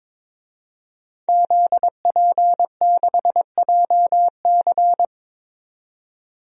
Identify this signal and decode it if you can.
Morse code, heard as a keyed tone, about 22 wpm, 700 Hz. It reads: ZP6JC
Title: DXCC-100 Call Signs - 5 Characters - 22wpm